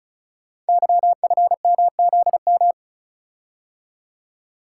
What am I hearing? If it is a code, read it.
Morse code, heard as a keyed tone, about 36 wpm, 690 Hz. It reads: YFMZM
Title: Code Groups - 5 Letters - 35wpm